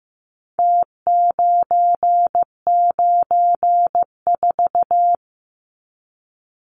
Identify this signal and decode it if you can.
Morse code, heard as a keyed tone, about 15 wpm, 700 Hz. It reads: T994
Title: Numbers - Contest Style - 4-Digit 15wpm